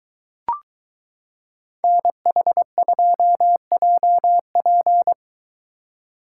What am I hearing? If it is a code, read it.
Morse code, heard as a keyed tone, about 23 wpm, 700 Hz. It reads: NH2JP